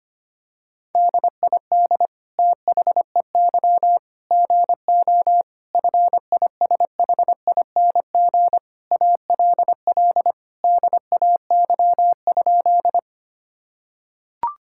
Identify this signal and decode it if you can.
Morse code, heard as a keyed tone, about 25 wpm, 700 Hz. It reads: DID THEY GO FISHING ALL DAY?
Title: Sentences from Top 1000 Words - Review Part 5 - 25wpm